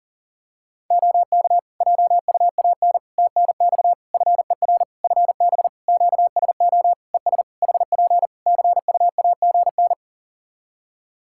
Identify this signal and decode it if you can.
Morse code, heard as a keyed tone, about 40 wpm, 700 Hz. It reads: OK JUAN TNX FER FB QSO ES HP CUAGN
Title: Practice QSO - Exchange 4 - Ending - 40wpm